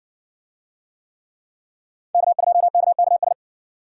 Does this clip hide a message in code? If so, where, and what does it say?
Yes: Morse code from 2.1 to 3.3 s, heard as a keyed tone, about 60 wpm, 700 Hz. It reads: K2KKS